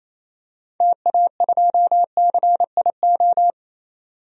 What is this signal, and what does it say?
Morse code, heard as a keyed tone, about 27 wpm, 700 Hz. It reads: TA2CIO
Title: DXCC-100 Call Signs - 6 Characters - 28wpm